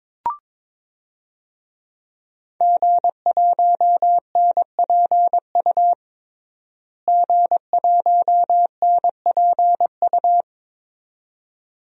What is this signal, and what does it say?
Morse code, heard as a keyed tone, about 22 wpm, 700 Hz. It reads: G1NPU G1NPU
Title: DXCC-100 Call Signs - 5 Characters - Repeated 2x - 22wpm